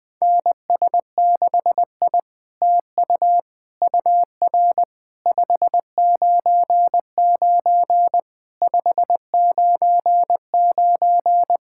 Morse code, heard as a keyed tone, about 20 wpm, 700 Hz. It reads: NS6I TU UR 599 599
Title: SOTA QSO Practice 20wpm